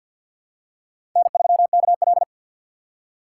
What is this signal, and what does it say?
Morse code, heard as a keyed tone, about 48 wpm, 700 Hz. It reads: N3KL
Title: USA Call Signs - 4 Characters - 2x Character Spacing - 50wpm